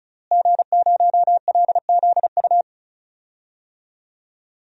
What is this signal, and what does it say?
Morse code, heard as a keyed tone, about 36 wpm, 700 Hz. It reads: G0LZU
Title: DXCC-100 Call Signs - 5 Characters - Repeated 3x - 35wpm